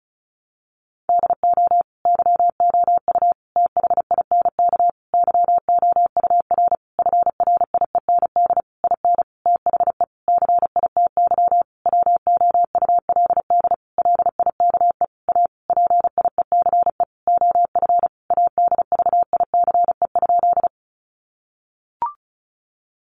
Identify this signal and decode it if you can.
Morse code, heard as a keyed tone, about 34 wpm, 700 Hz. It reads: DO YOU THINK YOUR FRIEND IN THE CITY WOULD LIKE A PIECE OF ADVICE?
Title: Sentences from Top 1100 Words 35wpm